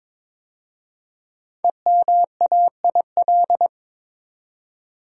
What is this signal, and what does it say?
Morse code, heard as a keyed tone, about 22 wpm, 690 Hz. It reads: EMAIL
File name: 5 Letter Words - First 3 Letters Given - 22wpm